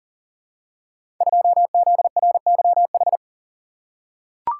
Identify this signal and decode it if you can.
Morse code, heard as a keyed tone, about 39 wpm, 700 Hz. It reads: 2ZRYH
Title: Code Groups - 5 Letter:Numbers - 40wpm